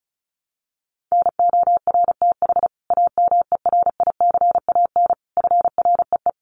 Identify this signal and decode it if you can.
Morse code, heard as a keyed tone, about 34 wpm, 700 Hz. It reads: NORTH AMERICAN FREE
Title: Sets of 5 Words Encore 35wpm